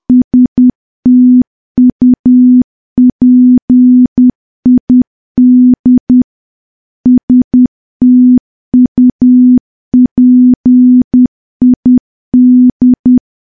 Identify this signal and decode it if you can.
Morse code, heard as a keyed tone, about 10 wpm, 250 Hz. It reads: STUPID STUPID
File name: morse / morse-training-10WPM